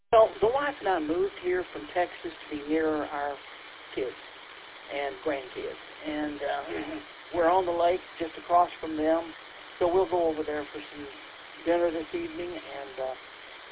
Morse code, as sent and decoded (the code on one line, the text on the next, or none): none